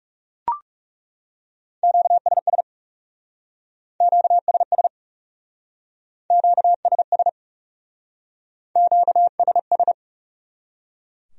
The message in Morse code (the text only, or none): QSS QSS QSS QSS